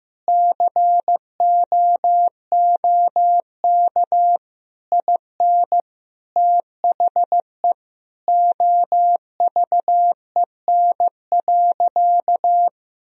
Morse code, heard as a keyed tone, about 15 wpm, 700 Hz. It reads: COOK IN THE OVEN.
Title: Sentences from Top 1000 Words - Review Part 4 - 15wpm